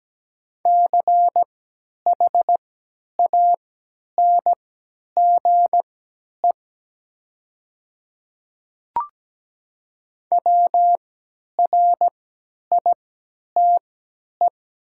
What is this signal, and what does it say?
Morse code, heard as a keyed tone, about 17 wpm, spaced as for 10 wpm, 700 Hz. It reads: CHANGE WRITE
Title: 3x Character Spacing - Top 200 Words - 17wpm